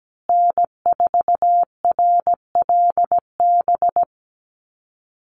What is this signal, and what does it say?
Morse code, heard as a keyed tone, about 17 wpm, 700 Hz. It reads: N4RLB
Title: USA Call Signs Repeated 2x 17wpm